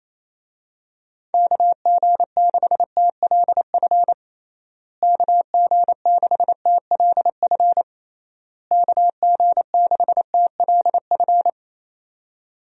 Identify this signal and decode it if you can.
Morse code, heard as a keyed tone, about 28 wpm, 700 Hz. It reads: KG6TLF KG6TLF KG6TLF
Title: USA Call Signs - 6 Characters - Repeated 3x - 28wpm